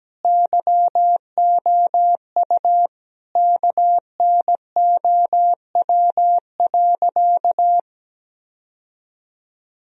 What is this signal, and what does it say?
Morse code, heard as a keyed tone, about 17 wpm, 700 Hz. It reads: YOU KNOW.
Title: Sentences from Top 1000 Words - Review Part 2 - 17wpm